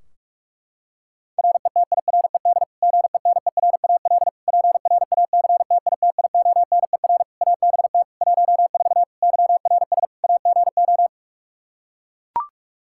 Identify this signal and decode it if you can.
Morse code, heard as a keyed tone, about 43 wpm, 700 Hz. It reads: RETIRED GENERAL PRACTITIONER ABT 14 YRS AGO
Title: Practice QSO - Exchange 3 - More Info - 45wpm